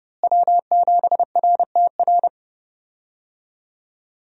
Morse code, heard as a keyed tone, about 30 wpm, 700 Hz. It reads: W7RTR
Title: USA Call Signs - 5 Characters - Repeated 3x - 30wpm